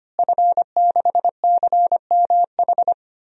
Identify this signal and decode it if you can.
Morse code, heard as a keyed tone, about 25 wpm, 700 Hz. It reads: F6CMH
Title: DXCC-100 Call Signs - 5 Characters - Repeated 3x - 25wpm